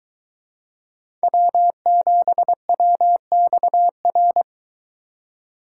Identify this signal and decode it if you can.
Morse code, heard as a keyed tone, about 22 wpm, 700 Hz. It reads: W7WXR